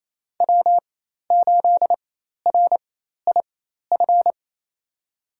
Morse code, heard as a keyed tone, about 27 wpm, spaced as for 13 wpm, 700 Hz. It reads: W8RIF
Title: USA Call Signs - 5 Characters - 4x Character Spacing - 28wpm